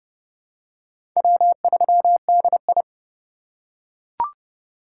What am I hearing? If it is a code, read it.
Morse code, heard as a keyed tone, about 30 wpm, 700 Hz. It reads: W3DI
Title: USA Call Signs - 4 Characters - 2x Character Spacing - 30wpm